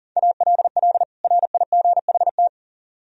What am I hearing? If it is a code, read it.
Morse code, heard as a keyed tone, about 52 wpm, 700 Hz. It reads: ALL RIGHT